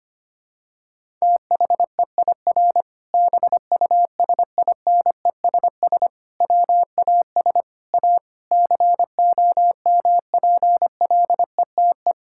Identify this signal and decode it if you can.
Morse code, heard as a keyed tone, about 25 wpm, 700 Hz. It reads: THEIR BUSINESS WAS A COMPLETE
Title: Sentences from Top 1000 Words - Review Part 6 - 25wpm